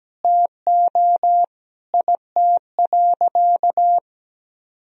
Morse code, heard as a keyed tone, about 17 wpm, 700 Hz. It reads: TO IT.
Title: Sentences from Top 1000 Words - Review Part 3 - 17wpm